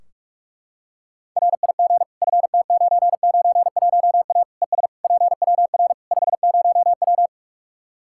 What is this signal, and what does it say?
Morse code, heard as a keyed tone, about 42 wpm, 700 Hz. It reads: RIG FT991A ES PWR 50W
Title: K1USN SST QSO Practice 45wpm